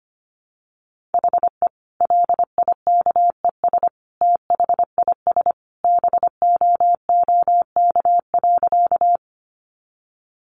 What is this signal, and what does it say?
Morse code, heard as a keyed tone, about 25 wpm, 700 Hz. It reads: HE LIKES THIS BOOK.
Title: Additional Easy Phrases 25wpm